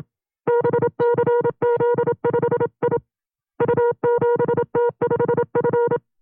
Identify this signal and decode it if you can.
Morse code, heard as a keyed tone, about 27 wpm, 480 Hz. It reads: BCZ5I U7T5F
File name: FreeFiles / CW-practice / 25WPM-side2